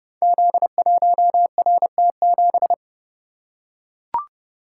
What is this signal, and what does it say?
Morse code, heard as a keyed tone, about 30 wpm, 700 Hz. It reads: Z1RT7
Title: Code Groups - 5 Letter:Numbers - 30wpm